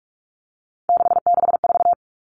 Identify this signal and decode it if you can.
Morse code, heard as a keyed tone, about 44 wpm, 700 Hz. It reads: 664
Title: Common Cut Numbers - 3-Digit 45wpm